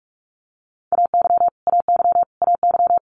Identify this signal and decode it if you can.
Morse code, heard as a keyed tone, about 46 wpm, 700 Hz. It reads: AY AY AY
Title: Multi-letter Phonemes Repeated 3x 45wpm